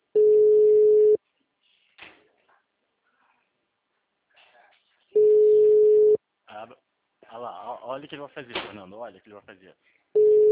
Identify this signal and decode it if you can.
Ringback tone (Europe, 425 Hz), with speech.
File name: out- -2024- - - .26529